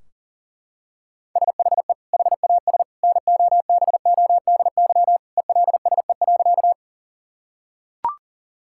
Morse code, heard as a keyed tone, about 40 wpm, 700 Hz. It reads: SHE HAS NOBODY ELSE.
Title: Sentences from Top 1600 Words 40wpm